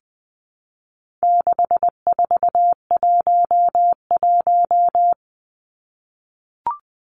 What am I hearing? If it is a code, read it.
Morse code, heard as a keyed tone, about 20 wpm, 700 Hz. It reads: 6411
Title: Numbers - 4-Digit 20wpm